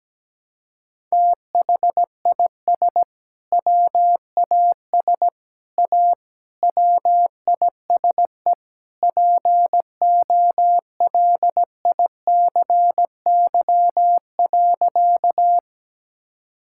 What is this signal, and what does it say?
Morse code, heard as a keyed tone, about 17 wpm, 700 Hz. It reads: THIS WAS A WISE POLICY.
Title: Sentences from Top 1600 Words 17wpm